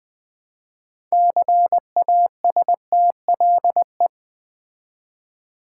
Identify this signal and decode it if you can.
Morse code, heard as a keyed tone, about 20 wpm, 700 Hz. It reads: CASTLE
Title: Build A Word - 3 Letters At A Time - 6-Letter Words - 20wpm